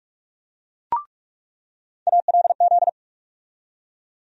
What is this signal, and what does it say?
Morse code, heard as a keyed tone, about 46 wpm, 700 Hz. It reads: ALZ
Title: USA Call Sign Postfixes - 3 Letters 45wpm